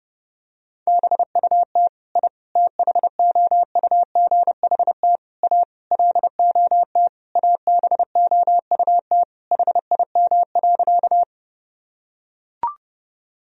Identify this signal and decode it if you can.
Morse code, heard as a keyed tone, about 32 wpm, 700 Hz. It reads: BUT I THOUGHT A LOT ABOUT HIM.
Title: Sentences from Top 1000 Words - Review Part 3 - 30wpm